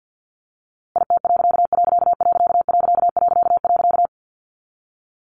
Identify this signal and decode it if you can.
Morse code, heard as a keyed tone, about 50 wpm, 700 Hz. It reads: IT......
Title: A Sound Of Thunder 50wpm